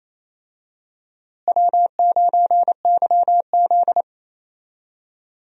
Morse code, heard as a keyed tone, about 28 wpm, 700 Hz. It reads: W9YZ
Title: USA Call Signs - 4 Characters - Repeated 3x - 28wpm